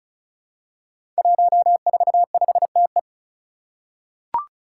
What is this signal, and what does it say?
Morse code, heard as a keyed tone, about 36 wpm, 700 Hz. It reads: 145TE
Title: Five Challenging Letter-Number Combinations 35wpm